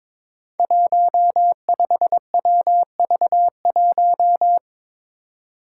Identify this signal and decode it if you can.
Morse code, heard as a keyed tone, about 22 wpm, 700 Hz. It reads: 15WV1
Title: Five Challenging Letter-Number Combinations 22wpm